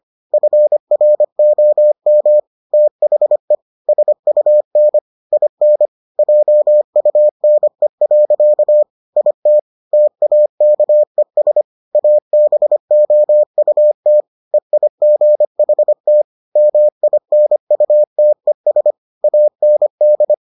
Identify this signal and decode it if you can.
Morse code, heard as a keyed tone, about 25 wpm, 600 Hz. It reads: FROM THE SUN IN JUNE. IT TAKES ABOUT EIGHT MINUTES AND